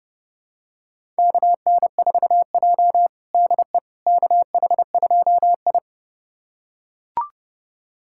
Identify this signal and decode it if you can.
Morse code, heard as a keyed tone, about 32 wpm, 700 Hz. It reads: KN4J DE KH2I
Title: USA Callsign DE Callsign - 4 Characters - 30wpm